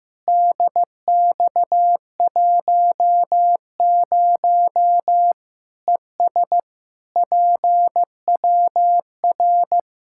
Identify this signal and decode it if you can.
Morse code, heard as a keyed tone, about 15 wpm, 700 Hz. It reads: DX10 ES PWR
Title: Practice QSO - Exchange 2 - Info - 15wpm